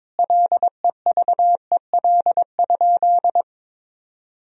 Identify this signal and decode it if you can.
Morse code, heard as a keyed tone, about 22 wpm, 700 Hz. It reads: LEVEL?
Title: Sentences from Top 1000 Words - Review Part 4 - 22wpm